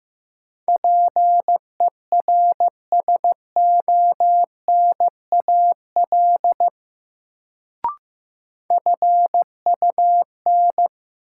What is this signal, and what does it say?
Morse code, heard as a keyed tone, about 15 wpm, 700 Hz. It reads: PERSONAL FUN
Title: Sentences from Top 700 Words - New Word Drill - 15wpm